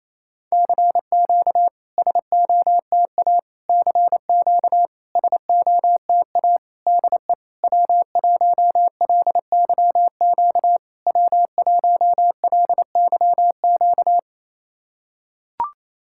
Morse code, heard as a keyed tone, about 27 wpm, 690 Hz. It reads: CQ SOTA CQ SOTA DE W1LYQ W1LYQ
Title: SOTA QSO Practice 28wpm